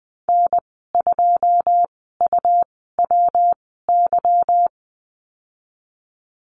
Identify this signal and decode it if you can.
Morse code, heard as a keyed tone, about 20 wpm, spaced as for 15 wpm, 700 Hz. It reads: N2UWY
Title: 167 Lesson 27g - Number 7 - Call Signs 20wpm